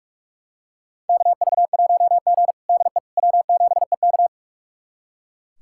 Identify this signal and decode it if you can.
Morse code, heard as a keyed tone, about 43 wpm, 700 Hz. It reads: KV1G DE W7EK
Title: USA Callsign DE Callsign - 4 Characters - 45wpm